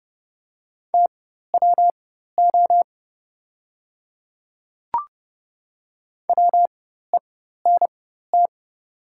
Morse code, heard as a keyed tone, about 30 wpm, spaced as for 14 wpm, 700 Hz. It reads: TWO WENT
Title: 4x Character Spacing - Top 300 Words - 30wpm